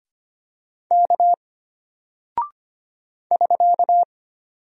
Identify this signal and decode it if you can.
Morse code, heard as a keyed tone, about 25 wpm, 700 Hz. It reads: K <SK>
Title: Single Character (Letter:Number:Punctuation:Prosign) Alt 25wpm